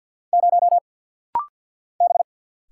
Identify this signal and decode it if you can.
Morse code, heard as a keyed tone, about 48 wpm, 700 Hz. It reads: 0 B
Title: Single Letter-Number - Rapid-Fire 50wpm